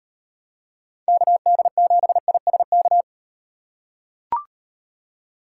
Morse code, heard as a keyed tone, about 38 wpm, 700 Hz. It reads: KD7ISK